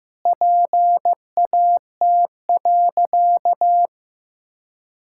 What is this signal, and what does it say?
Morse code, heard as a keyed tone, about 15 wpm, 700 Hz. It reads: PAT.
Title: Pangram Challenge v2 - 15wpm